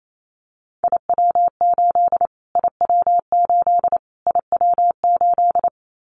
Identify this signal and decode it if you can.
Morse code, heard as a keyed tone, about 28 wpm, 700 Hz. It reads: IW8 IW8 IW8
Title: Callsign Prefixes for Top 25 DXCC - Repeated 3x - 28wpm